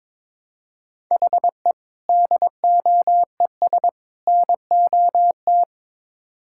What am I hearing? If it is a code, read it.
Morse code, heard as a keyed tone, about 21 wpm, 700 Hz. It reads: HE DOES NOT
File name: practice / 22wpm / Build A Phrase - 5-Word Phrases - 22wpm